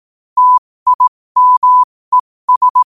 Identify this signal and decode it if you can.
Morse code, heard as a keyed tone, about 18 wpm, spaced as for 15 wpm, 1000 Hz. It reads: TIMES
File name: qotdfiles / 15wpm